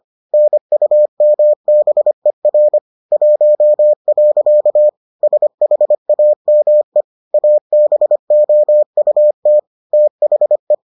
Morse code, heard as a keyed tone, about 25 wpm, 600 Hz. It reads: NUMBER 1. SHAME ABOUT THE